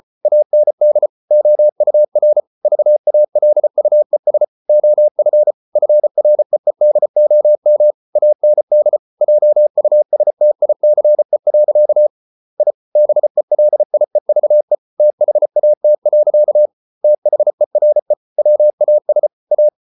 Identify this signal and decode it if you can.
Morse code, heard as a keyed tone, about 33 wpm, 600 Hz. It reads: AND OUR VALUES OF FREEDOM AND JUSTICE. I BELIEVE THAT. THERE WAS A